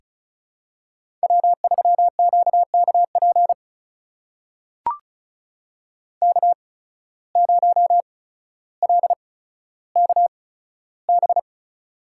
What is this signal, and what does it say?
Morse code, heard as a keyed tone, about 36 wpm, spaced as for 15 wpm, 700 Hz. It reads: W3QKP K0LKB
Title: USA Call Signs - 5 Characters - 8x Character Spacing - 35wpm